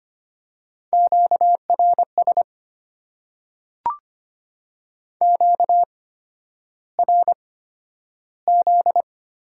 Morse code, heard as a keyed tone, about 25 wpm, spaced as for 11 wpm, 690 Hz. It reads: QRS QRZ
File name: Top 90 QSO Elements - 8x Character Spacing - 25wpm